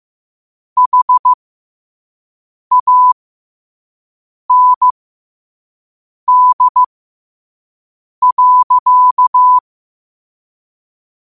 Morse code, heard as a keyed tone, about 15 wpm, spaced as for 5 wpm, 1000 Hz. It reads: HAND.
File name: qotdfiles / 5wpm